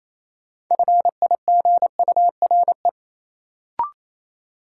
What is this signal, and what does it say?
Morse code, heard as a keyed tone, about 28 wpm, 700 Hz. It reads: FIGURE
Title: Sentences from Top 1000 Words - Review Part 2 - New Word Drill - 28wpm